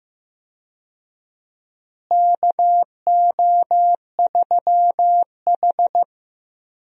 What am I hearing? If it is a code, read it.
Morse code, heard as a keyed tone, about 15 wpm, 700 Hz. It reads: KO3H
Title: USA Call Signs - 4 Characters - Repeated 3x - 15wpm